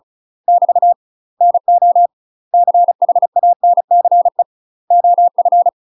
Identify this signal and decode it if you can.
Morse code, heard as a keyed tone, about 36 wpm, 700 Hz. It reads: <BT> NO CHANCE OF
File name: UK MASTER QSOs TRAINING / 35WPM / UK MASTER QSOs 0018